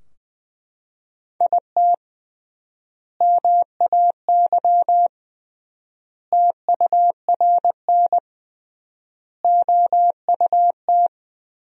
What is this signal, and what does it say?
Morse code, heard as a keyed tone, about 21 wpm, 700 Hz. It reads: IT MAY TURN OUT
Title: 3x Word-Spacing - Sentences from Top 400 Words - 20wpm